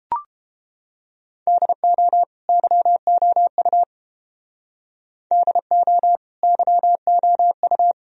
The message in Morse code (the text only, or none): DO YOU DO YOU